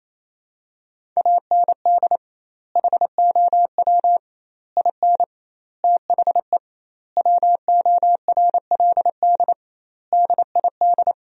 Morse code, heard as a keyed tone, about 28 wpm, 700 Hz. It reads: AND HOW IN THE WORLD DID
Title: 2x Word-Spacing - Sentences from Top 200 Words - 28wpm